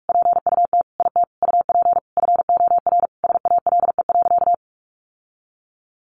Morse code, heard as a keyed tone, about 44 wpm, 700 Hz. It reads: PUT IT UP FOR SALE.
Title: Sentences from Top 1400 Words 45wpm